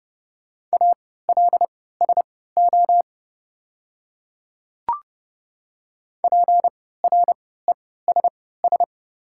Morse code, heard as a keyed tone, about 30 wpm, spaced as for 17 wpm, 690 Hz. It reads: ALSO PRESS
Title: 3x Character Spacing - Top 500 Words - 30wpm